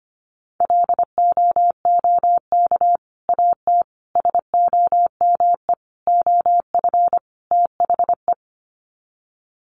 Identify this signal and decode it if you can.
Morse code, heard as a keyed tone, about 25 wpm, 700 Hz. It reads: LOOK AT SOME OF THE
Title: Sets of 5 Words Encore 25wpm